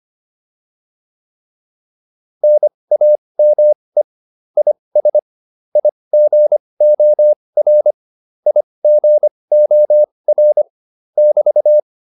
Morse code, heard as a keyed tone, about 25 wpm, spaced as for 20 wpm, 600 Hz. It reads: NAME IS IGOR IGOR <BT>